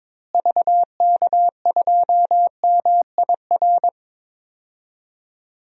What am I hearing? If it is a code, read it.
Morse code, heard as a keyed tone, about 22 wpm, 700 Hz. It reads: VK2MIR
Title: DXCC-100 Call Signs - 6 Characters - Repeated 2x - 22wpm